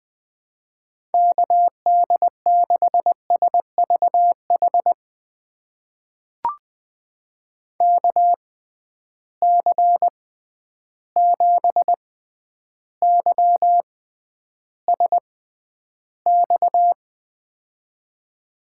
Morse code, heard as a keyed tone, about 20 wpm, spaced as for 10 wpm, 700 Hz. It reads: KD6SVH KC7YSX
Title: USA Call Signs - 6 Characters - 6x Character Spacing - 20wpm